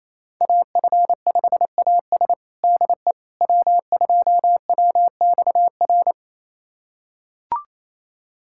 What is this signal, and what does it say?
Morse code, heard as a keyed tone, about 27 wpm, 700 Hz. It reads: AF5AS DE W2WXR
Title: USA Callsign DE Callsign - 5 Characters - 28wpm